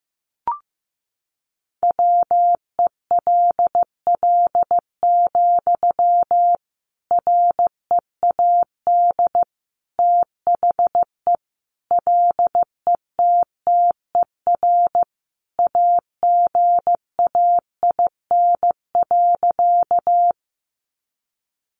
Morse code, heard as a keyed tone, about 15 wpm, 700 Hz. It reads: WELL, READ THE LETTER AGAIN.
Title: Sentences from Top 300 Words 15wpm